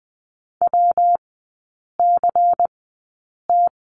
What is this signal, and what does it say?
Morse code, heard as a keyed tone, about 20 wpm, spaced as for 8 wpm, 700 Hz. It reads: WCT
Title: 114 Lesson 19e - Letter w - Three Characters 20wpm